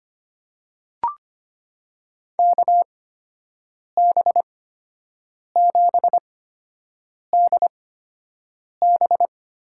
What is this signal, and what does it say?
Morse code, heard as a keyed tone, about 25 wpm, spaced as for 7 wpm, 700 Hz. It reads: KB7DB
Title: USA Call Signs - 6 Characters - 8x Character Spacing - 25wpm